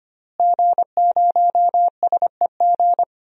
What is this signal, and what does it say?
Morse code, heard as a keyed tone, about 25 wpm, 700 Hz. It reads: G0SEG
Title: DXCC-100 Call Signs - 5 Characters - Repeated 3x - 25wpm